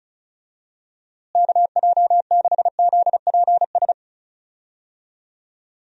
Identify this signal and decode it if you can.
Morse code, heard as a keyed tone, about 36 wpm, 700 Hz. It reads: KJ6ZPS
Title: USA Call Signs - 6 Characters - Repeated 2x - 35wpm